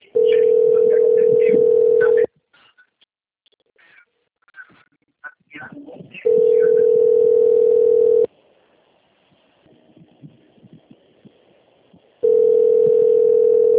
Ringback tone (North America), 440 + 480 Hz.
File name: out- - - - - .5046